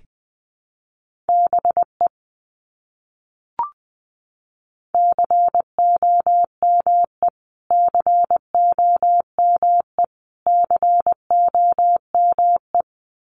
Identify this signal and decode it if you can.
Morse code, heard as a keyed tone, about 20 wpm, 700 Hz. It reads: BE COME COME COME
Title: Top 200 Words Repeated 20wpm